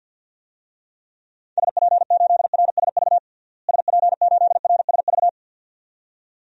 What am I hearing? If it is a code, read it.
Morse code, heard as a keyed tone, about 47 wpm, 700 Hz. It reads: SP8RSV SP8RSV
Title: DXCC-100 Call Signs - 6 Characters - Repeated 2x - 50wpm